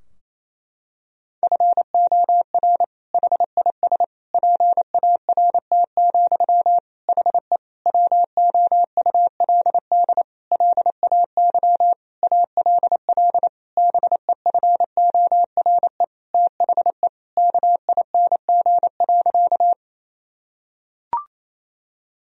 Morse code, heard as a keyed tone, about 28 wpm, 700 Hz. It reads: FOR HIS PART, HE WOULD LAY ALL BEFORE THE KING.
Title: Sentences from Top 1000 Words - Review Part 2 - 28wpm